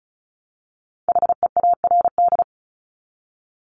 Morse code, heard as a keyed tone, about 34 wpm, 700 Hz. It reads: HEARD
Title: Top 500 Words Repeated 35wpm